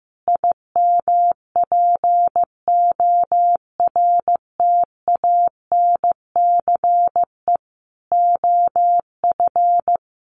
Morse code, heard as a keyed tone, about 15 wpm, 700 Hz. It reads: IMPORTANCE OF
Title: Sentences from Top 1300 Words 15wpm